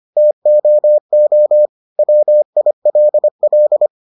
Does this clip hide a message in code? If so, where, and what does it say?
Yes: Morse code from 0.2 to 3.9 s, heard as a keyed tone, about 25 wpm, 600 Hz. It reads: TOO WILL